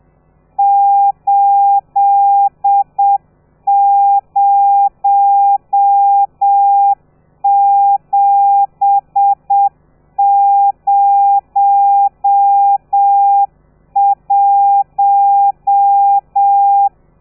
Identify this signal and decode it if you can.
Morse code, heard as a keyed tone, about 7 wpm, 800 Hz. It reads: 80701
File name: telegrafia / i2tfj / Lez13 30